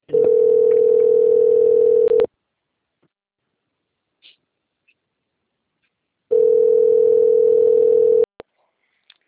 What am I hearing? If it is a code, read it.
Ringback tone (North America), 440 + 480 Hz.
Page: Itatiba II